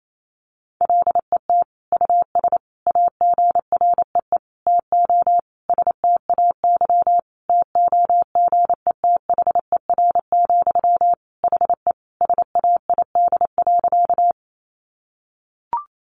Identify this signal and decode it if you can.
Morse code, heard as a keyed tone, about 28 wpm, 700 Hz. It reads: LET US AGREE TO STAY TOGETHER, HE SAID.
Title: Sentences from Top 1100 Words 28wpm